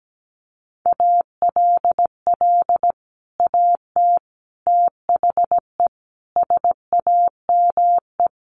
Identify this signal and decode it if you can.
Morse code, heard as a keyed tone, about 17 wpm, 700 Hz. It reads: ALL AT THE SAME
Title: Sets of 5 Words Encore2 17wpm